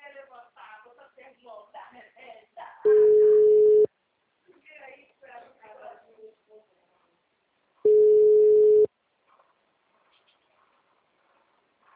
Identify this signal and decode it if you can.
Ringback tone (Europe, 425 Hz), with speech.
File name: out- -2024- - - .65524